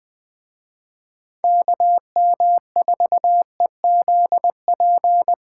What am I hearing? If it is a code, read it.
Morse code, heard as a keyed tone, about 20 wpm, 700 Hz. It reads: KM4EZP